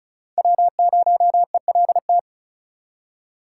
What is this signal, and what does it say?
Morse code, heard as a keyed tone, about 34 wpm, 700 Hz. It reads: W0ELT
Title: USA Call Signs - 5 Characters - 6x Character Spacing - 35wpm